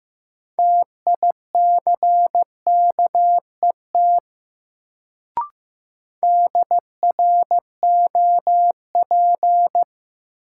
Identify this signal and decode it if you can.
Morse code, heard as a keyed tone, about 15 wpm, 700 Hz. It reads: TICKET DROP
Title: Sentences from Top 1400 Words - New Word Drill Alt - 15wpm